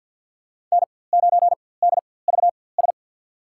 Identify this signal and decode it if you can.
Morse code, heard as a keyed tone, about 52 wpm, spaced as for 23 wpm, 700 Hz. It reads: N9DVS
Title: USA Call Signs - 5 Characters - 4x Character Spacing - 50wpm